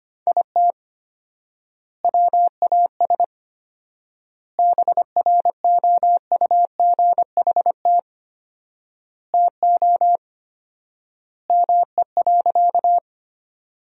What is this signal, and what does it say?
Morse code, heard as a keyed tone, about 25 wpm, 700 Hz. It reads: IT WAS BROUGHT TO ME.
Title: 4x Word-Spacing - Sentences from Top 500 Words - 25wpm